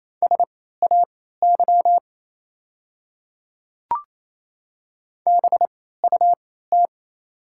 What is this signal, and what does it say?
Morse code, heard as a keyed tone, about 27 wpm, spaced as for 16 wpm, 700 Hz. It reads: SAY BUT
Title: 3x Character Spacing - Top 200 Words - 28wpm